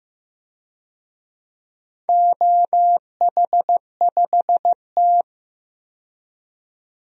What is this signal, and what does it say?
Morse code, heard as a keyed tone, about 15 wpm, 700 Hz. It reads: OH5T
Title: DXCC-100 Call Signs - 4 Characters - 15wpm